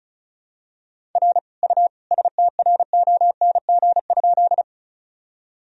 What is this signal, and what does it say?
Morse code, heard as a keyed tone, about 36 wpm, 700 Hz. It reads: R U STRONG?
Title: Sentences from Top 500 Words and Abbreviations 35wpm